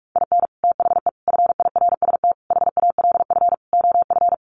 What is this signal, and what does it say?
Morse code, heard as a keyed tone, about 44 wpm, 700 Hz. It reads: IN THE FIRST HALF OF
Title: Sets of 5 Words 45wpm